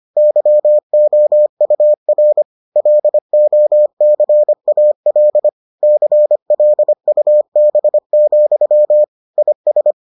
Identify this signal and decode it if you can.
Morse code, heard as a keyed tone, about 25 wpm, 600 Hz. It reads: YOUR LOCAL CLUB, IS